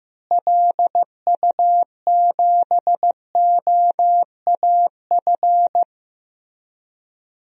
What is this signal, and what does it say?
Morse code, heard as a keyed tone, about 16 wpm, 700 Hz. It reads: LU7OAF
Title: DXCC-100 Call Signs - 6 Characters - Repeated 2x - 15wpm